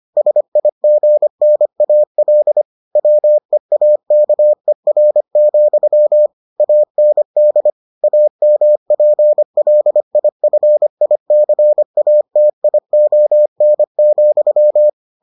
Morse code, heard as a keyed tone, about 25 wpm, 600 Hz. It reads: SIGNAL WEAKER, AND AMPLIFICATION,